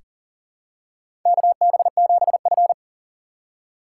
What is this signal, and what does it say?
Morse code, heard as a keyed tone, about 40 wpm, 700 Hz. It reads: KB7F
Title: USA Call Signs - 4 Characters - 6x Character Spacing - 40wpm